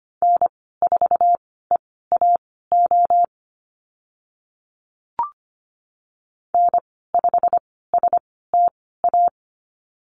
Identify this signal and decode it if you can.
Morse code, heard as a keyed tone, about 25 wpm, spaced as for 16 wpm, 700 Hz. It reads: N4EAO N5STA
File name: Lesson 15g - Number 5 - Call Signs 25wpm